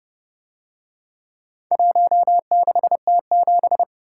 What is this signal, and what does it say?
Morse code, heard as a keyed tone, about 29 wpm, 700 Hz. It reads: 16T7